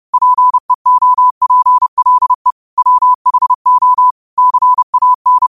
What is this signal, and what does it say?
Morse code, heard as a keyed tone, about 30 wpm, 1000 Hz. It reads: PEOPLE WHO CAN
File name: qotdfiles / 30wpm